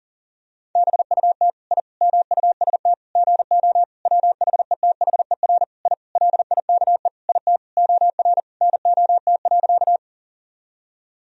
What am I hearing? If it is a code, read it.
Morse code, heard as a keyed tone, about 40 wpm, 700 Hz. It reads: BUT I MUST GO WHETHER I LIKE IT OR NOT.
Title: Sentences from Top 1000 Words - Review Part 3 - 40wpm